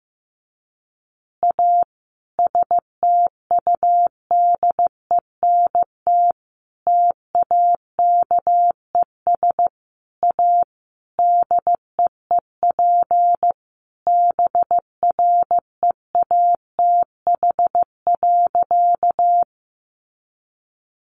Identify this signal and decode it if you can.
Morse code, heard as a keyed tone, about 15 wpm, 700 Hz. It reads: A STUDENT TAKES A DEEP BREATH.
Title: Additional Easy Phrases 15wpm